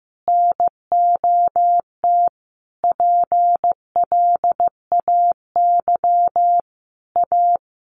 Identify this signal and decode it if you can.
Morse code, heard as a keyed tone, about 15 wpm, 700 Hz. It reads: NOT PLAY A
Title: Sentences from Top 1500 Words 15wpm